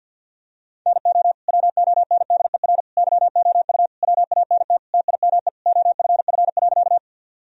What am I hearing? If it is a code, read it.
Morse code, heard as a keyed tone, about 50 wpm, 700 Hz. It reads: NO WONDER YOU WANT TIME OFF.
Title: Sentences from Top 1000 Words - Review Part 3 - 50wpm